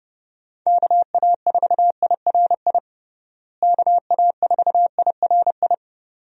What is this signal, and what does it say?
Morse code, heard as a keyed tone, about 32 wpm, 700 Hz. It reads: KA4IRI KA4IRI
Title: USA Call Signs - 6 Characters - Repeated 2x - 30wpm